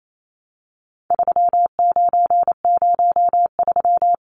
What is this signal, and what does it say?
Morse code, heard as a keyed tone, about 28 wpm, 700 Hz. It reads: 3903
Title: Numbers - 4-Digit 28wpm